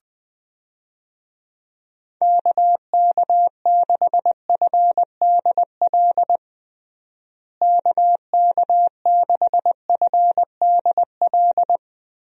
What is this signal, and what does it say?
Morse code, heard as a keyed tone, about 20 wpm, 700 Hz. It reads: KK6FDL KK6FDL
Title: USA Call Signs - 6 Characters - Repeated 3x - 20wpm